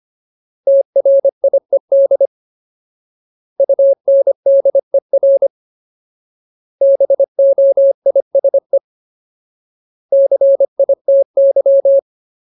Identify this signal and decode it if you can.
Morse code, heard as a keyed tone, about 25 wpm, 550 Hz. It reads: TRIED UNDER BOISE CITY
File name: Ultimate Word List / 25WPM Ultimate